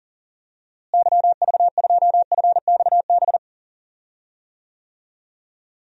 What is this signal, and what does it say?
Morse code, heard as a keyed tone, about 40 wpm, 700 Hz. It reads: YV2FXB
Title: DXCC-100 Call Signs - 6 Characters - Repeated 3x - 40wpm